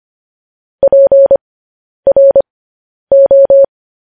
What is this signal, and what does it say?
Morse code, heard as a keyed tone, about 25 wpm, spaced as for 10 wpm, 550 Hz. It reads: PRO